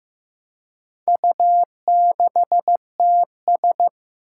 Morse code, heard as a keyed tone, about 15 wpm, 700 Hz. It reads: U6TS
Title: Four Challenging Letter-Number Combinations 15wpm